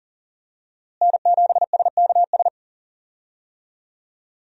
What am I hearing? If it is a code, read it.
Morse code, heard as a keyed tone, about 40 wpm, 700 Hz. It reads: N7SKS
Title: USA Call Signs - 5 Characters - Repeated 2x - 40wpm